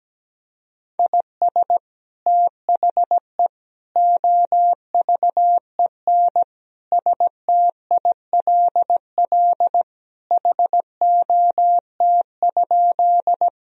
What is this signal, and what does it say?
Morse code, heard as a keyed tone, about 17 wpm, 700 Hz. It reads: IS THE OVEN STILL HOT?
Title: Sentences from Top 1000 Words - Review Part 4 - 17wpm